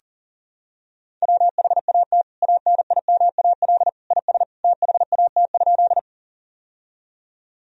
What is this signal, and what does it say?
Morse code, heard as a keyed tone, about 40 wpm, 700 Hz. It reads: WHAT ANIMAL IS THAT?
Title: Sentences from Top 1000 Words - Review Part 1 - 40wpm